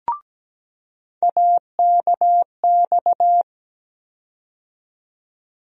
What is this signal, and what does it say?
Morse code, heard as a keyed tone, about 17 wpm, 700 Hz. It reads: AKX
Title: USA Call Sign Postfixes - 3 Letters 17wpm